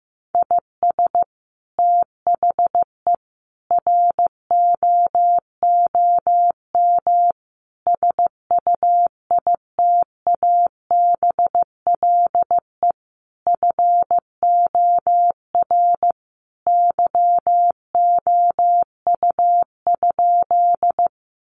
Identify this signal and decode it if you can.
Morse code, heard as a keyed tone, about 15 wpm, 700 Hz. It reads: IS THE ROOM SUITABLE FOR YOU?
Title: Sentences from Top 1400 Words 15wpm